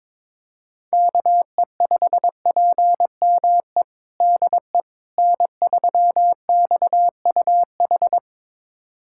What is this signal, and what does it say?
Morse code, heard as a keyed tone, about 22 wpm, 700 Hz. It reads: KE5PME DE N3XUH
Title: USA Callsign DE Callsign 22wpm